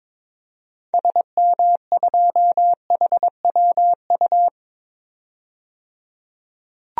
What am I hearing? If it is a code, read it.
Morse code, heard as a keyed tone, about 22 wpm, 700 Hz. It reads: SM2HWU
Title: DXCC-100 Call Signs - 6 Characters - 22wpm